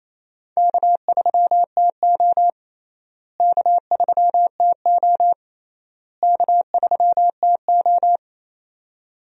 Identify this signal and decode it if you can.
Morse code, heard as a keyed tone, about 28 wpm, 700 Hz. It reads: K3TO K3TO K3TO
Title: USA Call Signs - 4 Characters - Repeated 3x - 28wpm